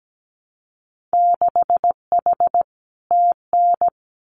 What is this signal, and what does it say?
Morse code, heard as a keyed tone, about 17 wpm, 700 Hz. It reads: 6H TN
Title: Winter Field Day Exchanges Encore 17wpm